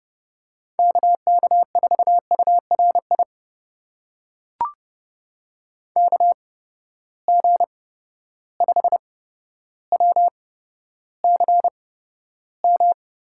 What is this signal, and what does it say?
Morse code, heard as a keyed tone, about 30 wpm, spaced as for 13 wpm, 700 Hz. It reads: KK4URI KG5WCM
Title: USA Call Signs - 6 Characters - 8x Character Spacing - 30wpm